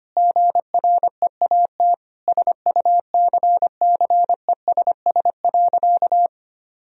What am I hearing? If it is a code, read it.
Morse code, heard as a keyed tone, about 25 wpm, 700 Hz. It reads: GREAT SUCCESS.
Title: Sentences from Top 1000 Words - Review Part 4 - 25wpm